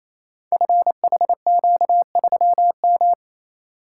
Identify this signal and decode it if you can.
Morse code, heard as a keyed tone, about 29 wpm, 700 Hz. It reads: FHQ3M
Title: Code Groups - 5 Letter:Numbers - 28wpm